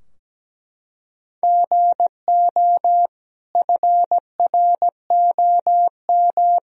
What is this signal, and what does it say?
Morse code, heard as a keyed tone, about 17 wpm, 700 Hz. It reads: GO FROM
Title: Sentences from Top 1000 Words - Review Part 2 - 17wpm